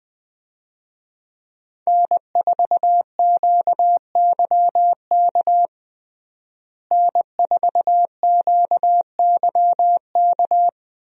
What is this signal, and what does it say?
Morse code, heard as a keyed tone, about 20 wpm, 700 Hz. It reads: N4QYK N4QYK
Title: USA Call Signs - 5 Characters - Repeated 2x - 20wpm